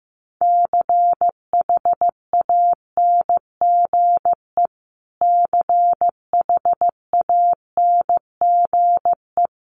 Morse code, heard as a keyed tone, about 15 wpm, 700 Hz. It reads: CHANGE CHANGE
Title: Top 500 Words Repeated 15wpm